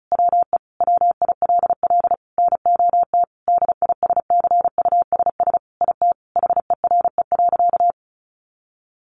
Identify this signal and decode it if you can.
Morse code, heard as a keyed tone, about 34 wpm, 700 Hz. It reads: WE WILL NOT DISCUSS IT HERE.
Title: Sentences from Top 1100 Words 35wpm